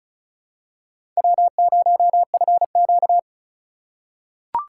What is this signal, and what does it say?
Morse code, heard as a keyed tone, about 36 wpm, 700 Hz. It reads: W0FQ
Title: USA Call Signs - 4 Characters - 2x Character Spacing - 35wpm